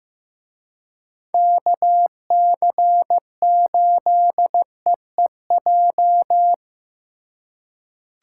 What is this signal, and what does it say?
Morse code, heard as a keyed tone, about 18 wpm, 700 Hz. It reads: KC8EEJ
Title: Speed Racing - USA 2x3 Call Signs 15wpm